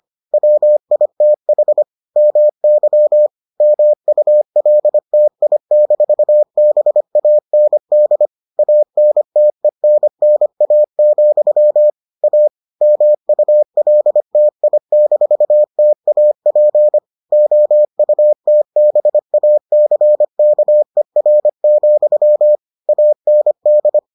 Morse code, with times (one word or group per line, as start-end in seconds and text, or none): 0.3-1.8 WITH
2.2-3.3 MY
3.6-8.3 MULTI-BAND
8.6-11.9 ANTENNA,
12.2-12.5 A
12.8-17.0 MULTI-TAP
17.3-22.6 OUTBACKER,
22.9-24.0 AND